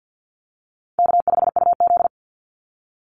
Morse code, heard as a keyed tone, about 50 wpm, 700 Hz. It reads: K5UZ
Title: Call Signs - 50wpm